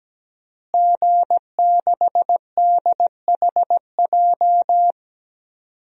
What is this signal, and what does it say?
Morse code, heard as a keyed tone, about 17 wpm, 700 Hz. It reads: G6DHJ
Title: DXCC-100 Call Signs - 5 Characters - Repeated 3x - 17wpm